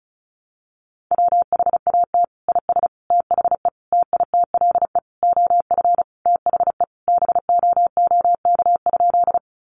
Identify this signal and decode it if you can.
Morse code, heard as a keyed tone, about 34 wpm, 700 Hz. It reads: WHAT IS THE TITLE OF THE BOOK?
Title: Sentences from Top 1200 Words 35wpm